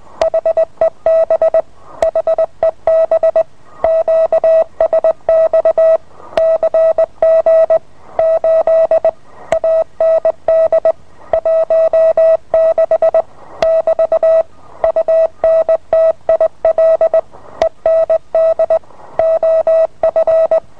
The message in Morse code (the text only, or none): HEB HEB QSX CG 8 AND 16 <BT> UNTIL END OF